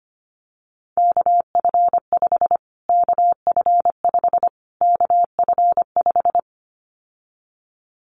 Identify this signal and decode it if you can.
Morse code, heard as a keyed tone, about 25 wpm, 700 Hz. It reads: KF5 KF5 KF5
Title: Callsign Prefixes for Top 25 DXCC - Repeated 3x - 25wpm